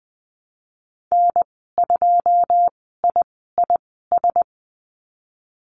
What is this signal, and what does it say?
Morse code, heard as a keyed tone, about 19 wpm, spaced as for 14 wpm, 700 Hz. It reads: N2IIS
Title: 088 Lesson 15g - Number 5 - Call Signs 20wpm